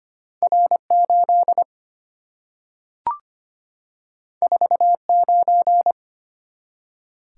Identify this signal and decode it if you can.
Morse code, heard as a keyed tone, about 25 wpm, 700 Hz. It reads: R8 49
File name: Two Letter-Number Combinations - 25wpm